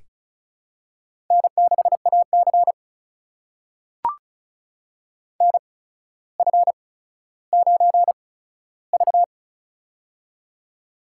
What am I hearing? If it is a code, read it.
Morse code, heard as a keyed tone, about 36 wpm, spaced as for 15 wpm, 700 Hz. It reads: N6AC NF9V
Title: USA Call Signs - 4 Characters - 8x Character Spacing - 35wpm